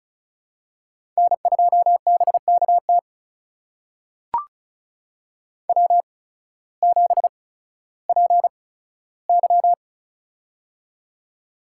Morse code, heard as a keyed tone, about 36 wpm, spaced as for 16 wpm, 700 Hz. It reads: N2BKT W7PY
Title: USA Call Signs - 8x Character Spacing - 35wpm